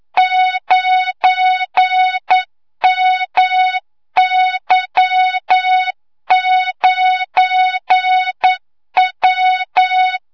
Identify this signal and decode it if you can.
Morse code, heard as a keyed tone, about 9 wpm, 3700 Hz, with noise. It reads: 9MY9W
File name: Morze-HA2MN / hangp / 2 06 GYAKORLAT